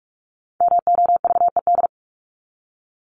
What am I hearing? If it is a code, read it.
Morse code, heard as a keyed tone, about 44 wpm, 700 Hz. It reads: MOVED
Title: ED Suffix - Top 100 Words with Base Words 45wpm